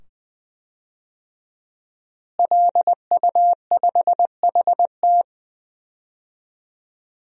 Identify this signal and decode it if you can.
Morse code, heard as a keyed tone, about 20 wpm, 700 Hz. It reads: LU5HT
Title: DXCC-100 Call Signs - 5 Characters - Repeated 2x - 20wpm